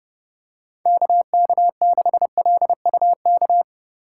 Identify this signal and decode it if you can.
Morse code, heard as a keyed tone, about 30 wpm, 700 Hz. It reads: KK6LUK
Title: USA Call Signs - 6 Characters - 2x Character Spacing - 30wpm